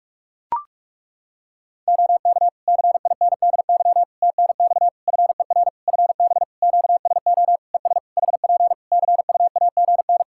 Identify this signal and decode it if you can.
Morse code, heard as a keyed tone, about 41 wpm, 700 Hz. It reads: OK CINDY TNX FER FB QSO ES HP CUAGN
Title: K1USN SST QSO Practice 45wpm